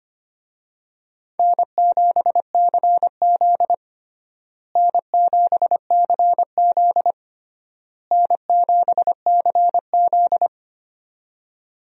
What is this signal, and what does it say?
Morse code, heard as a keyed tone, about 25 wpm, 700 Hz. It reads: N7CZ N7CZ N7CZ
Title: USA Call Signs - 4 Characters - Repeated 3x - 25wpm